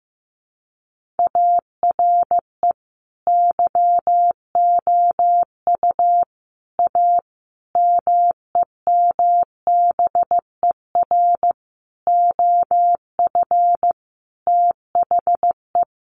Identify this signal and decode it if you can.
Morse code, heard as a keyed tone, about 15 wpm, 700 Hz. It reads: ARE YOU A MEMBER OF THE
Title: Sentences from Top 1400 Words 15wpm